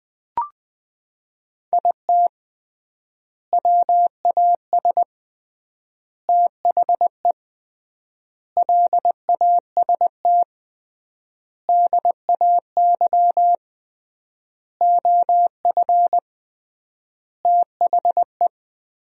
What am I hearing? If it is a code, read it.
Morse code, heard as a keyed tone, about 21 wpm, 700 Hz. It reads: IT WAS THE LAST DAY OF THE
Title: 3x Word-Spacing - Sentences from Top 300 Words - 20wpm